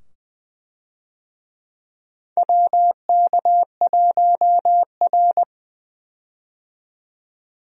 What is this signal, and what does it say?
Morse code, heard as a keyed tone, about 20 wpm, 700 Hz. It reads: WK1R
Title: USA Call Signs - 4 Characters - 20wpm